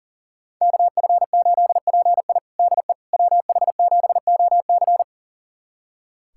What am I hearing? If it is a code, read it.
Morse code, heard as a keyed tone, about 40 wpm, 700 Hz. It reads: KF8PI DE WH7OC
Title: USA Callsign DE Callsign - 5 Characters - 40wpm